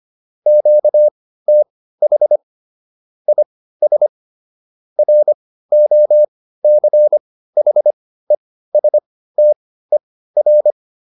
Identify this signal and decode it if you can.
Morse code, heard as a keyed tone, about 25 wpm, spaced as for 15 wpm, 600 Hz. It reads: QTH IS ROCHESTER